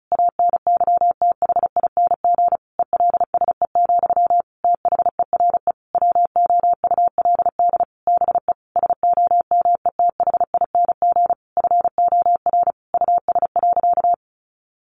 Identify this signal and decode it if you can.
Morse code, heard as a keyed tone, about 34 wpm, 700 Hz. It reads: ANYTHING ELSE, THERE WOULD BE SOMETHING FOR US.
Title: Sentences from Top 1200 Words 35wpm